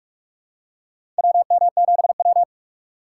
Morse code, heard as a keyed tone, about 43 wpm, 700 Hz. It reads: WM7W
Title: USA Call Signs - 4 Characters - 6x Character Spacing - 45wpm